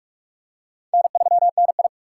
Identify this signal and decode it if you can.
Morse code, heard as a keyed tone, about 44 wpm, 700 Hz. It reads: N3NI
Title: USA Call Signs - 6x Character Spacing - 45wpm